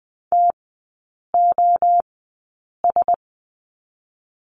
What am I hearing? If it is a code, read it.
Morse code, heard as a keyed tone, about 20 wpm, spaced as for 8 wpm, 700 Hz. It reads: TOS
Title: 035 Lesson 7e - Letter s - Three Characters 20wpm